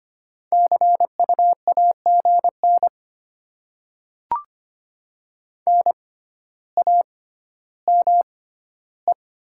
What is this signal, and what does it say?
Morse code, heard as a keyed tone, about 25 wpm, spaced as for 13 wpm, 700 Hz. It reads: CU AGN NAME
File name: Top 20 QSO Elements - 6x Character Spacing - 25wpm